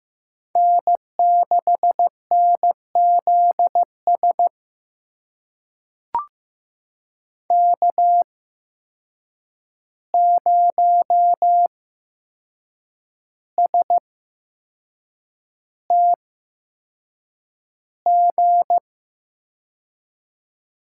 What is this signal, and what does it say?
Morse code, heard as a keyed tone, about 15 wpm, spaced as for 6 wpm, 700 Hz. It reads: N6NZS K0STG
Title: USA Call Signs - 5 Characters - 8x Character Spacing - 15wpm